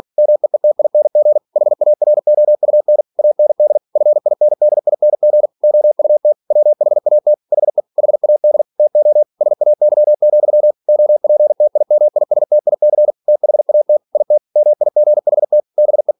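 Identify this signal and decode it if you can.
Morse code, heard as a keyed tone, about 46 wpm, 600 Hz. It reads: MEETING HAROUN AND FINDING OUT WHAT HE HAD TO SAY, OPTIMISTIC THAT IT MIGHT BE
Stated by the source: Practice text from an English novel sent at 47 WPM.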